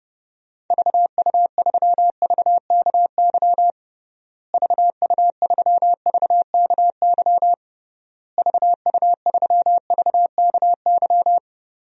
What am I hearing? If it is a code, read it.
Morse code, heard as a keyed tone, about 30 wpm, 700 Hz. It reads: VU3VKY VU3VKY VU3VKY
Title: DXCC-100 Call Signs - 6 Characters - Repeated 3x - 30wpm